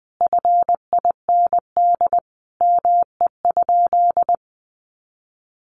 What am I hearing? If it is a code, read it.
Morse code, heard as a keyed tone, about 20 wpm, 700 Hz. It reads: FIND ME?
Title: Sentences from Top 1400 Words 20wpm